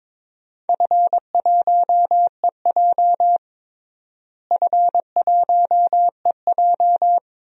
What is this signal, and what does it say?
Morse code, heard as a keyed tone, about 22 wpm, 700 Hz. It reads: F1EJ F1EJ
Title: DXCC-100 Call Signs - 4 Characters - Repeated 2x - 22wpm